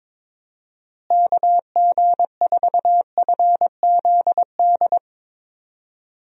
Morse code, heard as a keyed tone, about 22 wpm, 700 Hz. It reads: KG4FZD
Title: USA Call Signs - 6 Characters - Repeated 3x - 22wpm